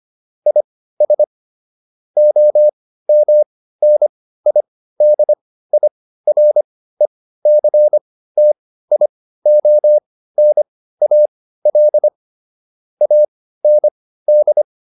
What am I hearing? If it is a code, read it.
Morse code, heard as a keyed tone, about 25 wpm, spaced as for 15 wpm, 600 Hz. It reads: IS OMNIDIRECTIONAL AND